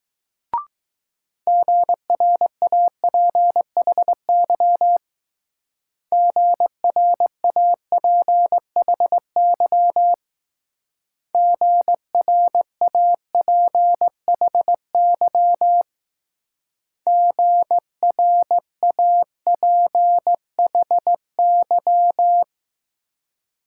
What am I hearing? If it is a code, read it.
Morse code, heard as a keyed tone, about 19 wpm, 700 Hz. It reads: GRAPHY GRAPHY GRAPHY GRAPHY